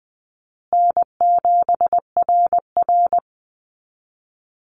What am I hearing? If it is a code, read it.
Morse code, heard as a keyed tone, about 20 wpm, 700 Hz. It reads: N7RR
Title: Call Signs Repeated 2x - 20wpm